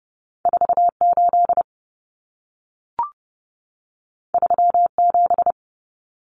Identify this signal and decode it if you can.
Morse code, heard as a keyed tone, about 30 wpm, 700 Hz. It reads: 48 37
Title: Numbers - 2-Digit 30wpm